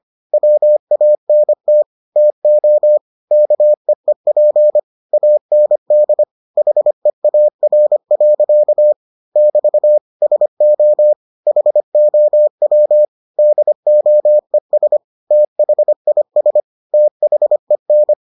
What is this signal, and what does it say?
Morse code, heard as a keyed tone, about 25 wpm, 600 Hz. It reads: WANT TO KEEP AND HEAR. <BT> SO HOW DOES THIS THEN